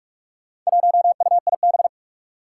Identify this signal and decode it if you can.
Morse code, heard as a keyed tone, about 44 wpm, 700 Hz. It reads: 1UIB
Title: Four Challenging Letter-Number Combinations 45wpm